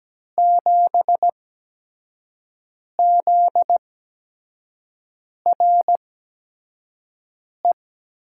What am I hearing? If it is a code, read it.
Morse code, heard as a keyed tone, about 17 wpm, spaced as for 5 wpm, 700 Hz. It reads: 7ZRE
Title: USA Call Signs - 5 Characters - 8x Character Spacing - 17wpm